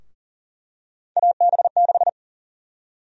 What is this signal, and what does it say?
Morse code, heard as a keyed tone, about 40 wpm, 700 Hz. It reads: AB6
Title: USA Call Sign Prefixes - Repeated 2x - 40wpm